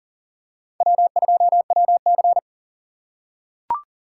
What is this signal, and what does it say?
Morse code, heard as a keyed tone, about 40 wpm, 700 Hz. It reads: W2WC
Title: USA Call Signs - 4 Characters - 3x Character Spacing - 40wpm